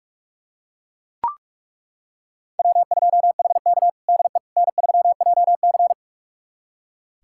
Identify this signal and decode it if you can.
Morse code, heard as a keyed tone, about 42 wpm, 700 Hz. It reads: W2HK DE N3JC
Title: USA Callsign DE Callsign - 4 Characters - 45wpm